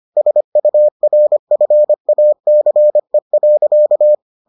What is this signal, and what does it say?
Morse code, heard as a keyed tone, about 25 wpm, 600 Hz. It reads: SURFACE.